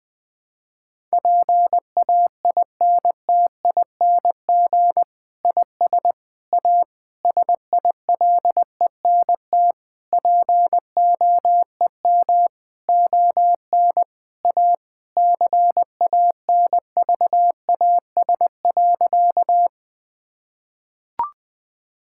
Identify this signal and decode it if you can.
Morse code, heard as a keyed tone, about 20 wpm, 700 Hz. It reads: PAINTING IS A SILENT POEM ON A CANVAS.
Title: Difficult Sentences with Hints - 20wpm